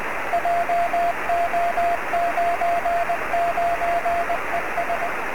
Morse code, heard as a keyed tone, about 20 wpm, 700 Hz, with radio noise. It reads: JO99ES
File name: sm7lcb / 02-sm0eui 23cm